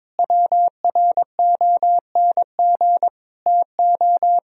Morse code, heard as a keyed tone, about 22 wpm, 700 Hz. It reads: WRONG TO
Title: Sentences from Top 1000 Words - Review Part 1 - 22wpm